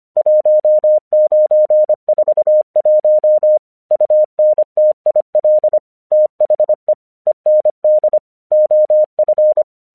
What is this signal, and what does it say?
Morse code, heard as a keyed tone, about 25 wpm, 610 Hz. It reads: 1941 UNTIL THE END OF
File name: 1 English 25wpm / AQC09